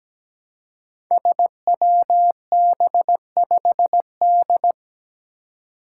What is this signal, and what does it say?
Morse code, heard as a keyed tone, about 17 wpm, 700 Hz. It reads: SWB5D
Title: Five Challenging Letter-Number Combinations 17wpm